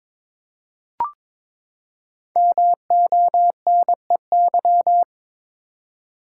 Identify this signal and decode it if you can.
Morse code, heard as a keyed tone, about 22 wpm, 700 Hz. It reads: MONEY
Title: Five Letter Words - 22wpm